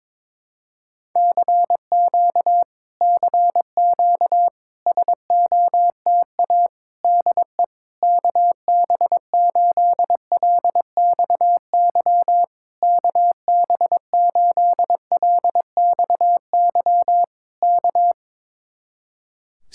Morse code, heard as a keyed tone, about 22 wpm, 700 Hz. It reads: CQ CQ SOTA DE KB8LXY KB8LXY K
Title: Speed-Racing Common QSO Phrases Encore - 22wpm